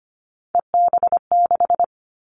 Morse code, heard as a keyed tone, about 25 wpm, 700 Hz. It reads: EB6
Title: Callsign Prefixes for Top 50 DXCC - 25wpm